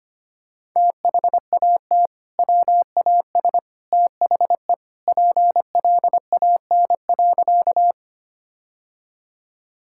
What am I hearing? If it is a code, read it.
Morse code, heard as a keyed tone, about 25 wpm, 700 Hz. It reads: THAT WAS THE PLAN.
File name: practice / 25wpm / Sentences with 2-Word Phrases - 25wpm